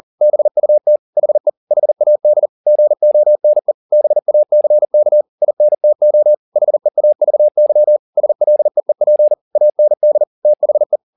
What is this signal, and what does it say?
Morse code, heard as a keyed tone, about 40 wpm, 600 Hz. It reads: BUT HE HAD GONE BACK INTO HEAVY SLEEP AND THE
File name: the old man and the sea 2 40wpm